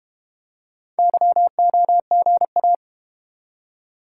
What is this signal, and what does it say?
Morse code, heard as a keyed tone, about 33 wpm, 700 Hz. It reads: YOGA